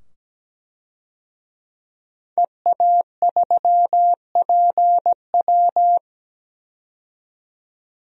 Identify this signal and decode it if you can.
Morse code, heard as a keyed tone, about 17 wpm, 700 Hz. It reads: EA3PW
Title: DXCC-100 Call Signs - 5 Characters - 17wpm